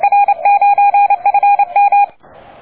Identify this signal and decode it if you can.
Morse code, heard as a keyed tone, about 28 wpm, 810 Hz. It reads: R9FM
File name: D4C@10CW